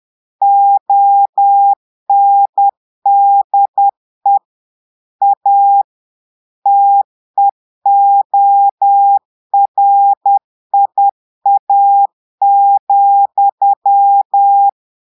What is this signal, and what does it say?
Morse code, heard as a keyed tone, about 10 wpm, 800 Hz. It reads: ONDE A TEORIA,
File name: Portugais - texte en clair 1 - vitesse - 10 mots minute